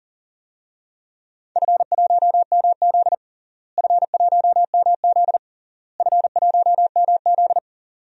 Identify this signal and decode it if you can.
Morse code, heard as a keyed tone, about 40 wpm, 700 Hz. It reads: F1MZ F1MZ F1MZ
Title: DXCC-100 Call Signs - 4 Characters - Repeated 3x - 40wpm